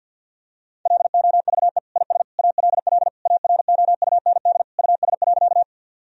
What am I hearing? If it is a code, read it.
Morse code, heard as a keyed tone, about 52 wpm, 700 Hz. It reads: LOVE IS ALL AROUND US.
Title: Sentences from Top 1000 Words - Review Part 3 - 50wpm